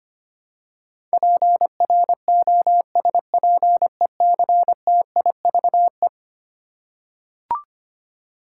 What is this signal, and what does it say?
Morse code, heard as a keyed tone, about 25 wpm, 700 Hz. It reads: PROSPECTIVE
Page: morse-code-ninja-practice-25wpm